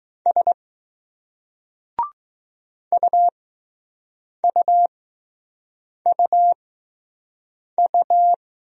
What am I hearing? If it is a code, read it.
Morse code, heard as a keyed tone, about 20 wpm, spaced as for 6 wpm, 700 Hz. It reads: S UUUU